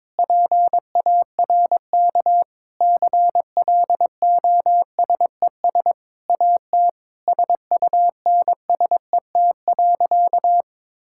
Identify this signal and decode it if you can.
Morse code, heard as a keyed tone, about 22 wpm, 700 Hz. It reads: PARK CLOSES AT SUNSET.
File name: Sentences with 7-Word Phrases - 22wpm